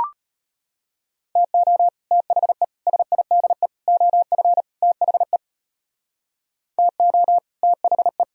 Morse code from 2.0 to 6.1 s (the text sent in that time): THE SIDE OF THE